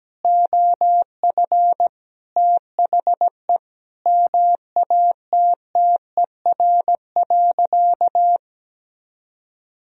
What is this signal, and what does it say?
Morse code, heard as a keyed tone, about 17 wpm, 700 Hz. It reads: OF THE MATTER.
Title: Sentences from Top 1000 Words - Review Part 3 - 17wpm